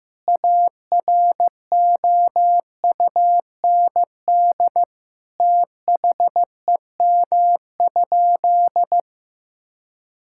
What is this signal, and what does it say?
Morse code, heard as a keyed tone, about 15 wpm, 700 Hz. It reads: AROUND THEM?
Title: Sentences from Top 1000 Words - Review Part 5 - 15wpm